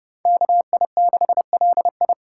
Morse code, heard as a keyed tone, about 28 wpm, 700 Hz. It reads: KI6LI